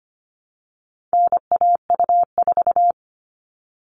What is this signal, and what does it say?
Morse code, heard as a keyed tone, about 25 wpm, 700 Hz. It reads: NAU4
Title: All Cut Numbers - 4-Digit 25wpm